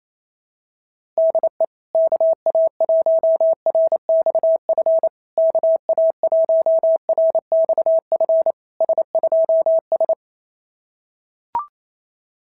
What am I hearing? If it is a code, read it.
Morse code, heard as a keyed tone, about 28 wpm, 660 Hz. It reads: DE KA1RXF KA1RXF S2S
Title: SOTA QSO Practice 28wpm